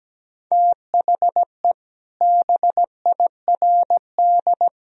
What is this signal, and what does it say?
Morse code, heard as a keyed tone, about 17 wpm, 700 Hz. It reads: THE BIRD
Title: Sentences from Top 1000 Words - Review Part 2 - 17wpm